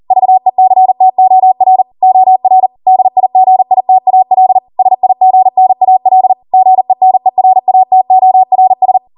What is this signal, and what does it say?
Morse code, heard as a keyed tone, about 40 wpm, 750 Hz. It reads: VECTOR OR DIGITAL SIGNAL GENERATORS